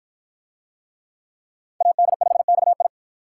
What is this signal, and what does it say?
Morse code, heard as a keyed tone, about 46 wpm, 700 Hz. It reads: AD5XI